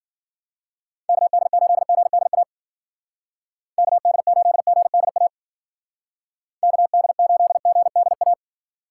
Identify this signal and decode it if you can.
Morse code, heard as a keyed tone, about 50 wpm, 700 Hz. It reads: KD8GDA KD8GDA KD8GDA
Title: Speed Racing - USA Call Signs 40wpm